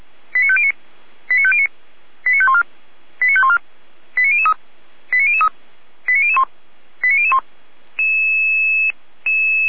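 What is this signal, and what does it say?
Five-tone selective call (ZVEI), heard as quick runs of five tones, tones of about 70 ms. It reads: 89509 89509 89514 89514 89003 89003 89001 89001
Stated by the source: ZVEI five-tone selective call received on SDR, from fire brigade feuerwehr-orlamuende.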